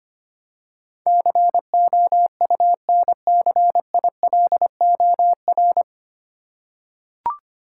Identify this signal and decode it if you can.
Morse code, heard as a keyed tone, about 25 wpm, 700 Hz. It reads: COUNCILOR
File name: practice / 25wpm / 9 Letter Words - First 3 Letters Given - 25wpm